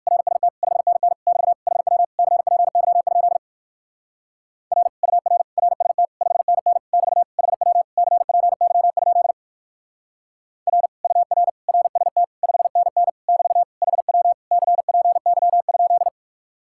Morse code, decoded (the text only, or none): RST 5NN <BT> HW CPY? R UR RST 5NN <BT> HW CPY? R UR RST 5NN <BT> HW CPY?